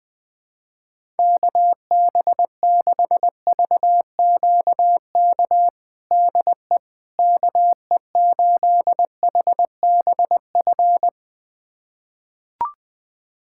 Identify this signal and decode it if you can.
Morse code, heard as a keyed tone, about 21 wpm, 700 Hz. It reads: KB6VQK DE KE8HBF
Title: USA Callsign DE Callsign - 6 Characters - 20wpm